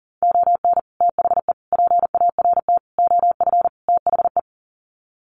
Morse code, heard as a keyed tone, about 40 wpm, 700 Hz. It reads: ON THE PART OF THE
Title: Sets of 5 Words 40wpm